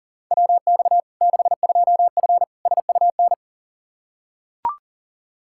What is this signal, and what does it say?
Morse code, heard as a keyed tone, about 40 wpm, 700 Hz. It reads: WX 62F SUN
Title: Practice QSO - Exchange 2 - Info - 40wpm